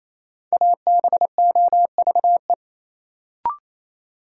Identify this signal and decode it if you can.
Morse code, heard as a keyed tone, about 28 wpm, 700 Hz. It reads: ABOVE
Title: Sentences from Top 1000 Words - Review Part 2 - New Word Drill - 28wpm